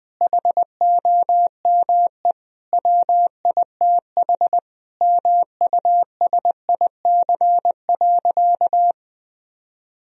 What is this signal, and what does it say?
Morse code, heard as a keyed tone, about 21 wpm, 700 Hz. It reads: HOME WITH MUSIC.
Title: Sentences from Top 1000 Words - Review Part 1 - 20wpm